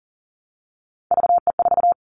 Morse code, heard as a keyed tone, about 39 wpm, 700 Hz. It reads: VE4
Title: Callsign Prefixes for Top 50 DXCC - 40wpm